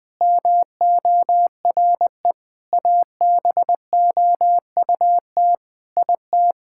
Morse code, heard as a keyed tone, about 20 wpm, 700 Hz. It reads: MORE ABOUT IT
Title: Sentences with 8-Word Phrases - 20wpm